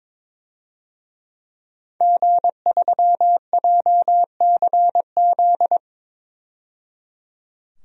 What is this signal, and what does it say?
Morse code, heard as a keyed tone, about 22 wpm, 700 Hz. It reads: G3JCZ
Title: DXCC-100 Call Signs - 5 Characters - 22wpm